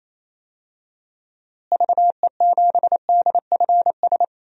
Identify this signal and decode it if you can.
Morse code, heard as a keyed tone, about 28 wpm, 700 Hz. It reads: VE7DFS
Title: DXCC-100 Call Signs - 6 Characters - Repeated 3x - 28wpm